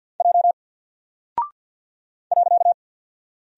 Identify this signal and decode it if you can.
Morse code, heard as a keyed tone, about 48 wpm, 700 Hz. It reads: J .
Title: Single Character (Letter:Number:Punctuation:Prosign) 50wpm